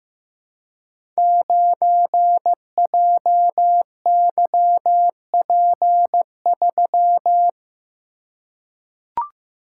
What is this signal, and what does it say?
Morse code, heard as a keyed tone, about 15 wpm, 700 Hz. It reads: 9JYP3
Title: Code Groups - 5 Letter:Numbers - 15wpm